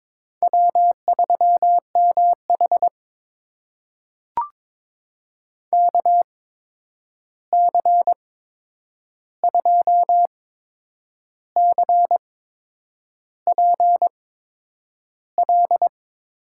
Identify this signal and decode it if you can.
Morse code, heard as a keyed tone, about 21 wpm, spaced as for 8 wpm, 700 Hz. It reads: W3MH KC2CPL
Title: USA Call Signs - 8x Character Spacing - 22wpm